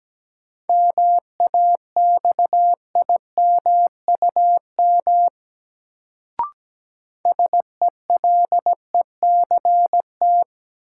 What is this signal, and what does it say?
Morse code, heard as a keyed tone, about 17 wpm, 700 Hz. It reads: MAXIMUM SELECT
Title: Sentences from Top 1300 Words - New Word Drill - 17wpm